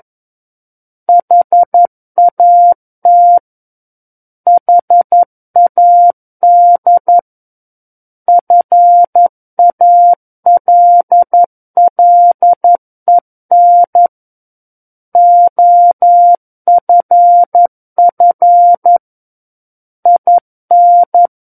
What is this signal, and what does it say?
Morse code, heard as a keyed tone, about 11 wpm, 700 Hz. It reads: HAT HAD FALLEN OFF IN
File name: Book One / 02-Chapter 2 - 11 WPM